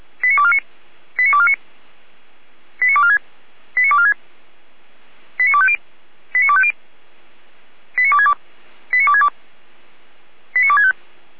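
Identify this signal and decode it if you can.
Five-tone selective call (ZVEI), heard as quick runs of five tones, tones of about 70 ms. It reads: 89249 89249 89246 89246 89250 89250 89262 89262 89265
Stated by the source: ZVEI five-tone call received by SDR, served by feuerwehr-orlamuende.org fire brigade.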